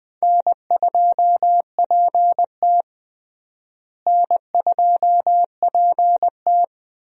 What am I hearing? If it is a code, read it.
Morse code, heard as a keyed tone, about 20 wpm, 700 Hz. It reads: N2PT N2PT
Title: USA Call Signs - 4 Characters - Repeated 2x - 20wpm